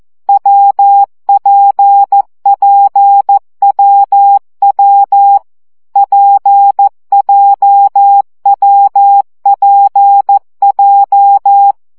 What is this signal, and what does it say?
Morse code, heard as a keyed tone, about 14 wpm, 800 Hz. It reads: WPPWW PJWPJ
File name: CW gyakorlas w p j 12 baud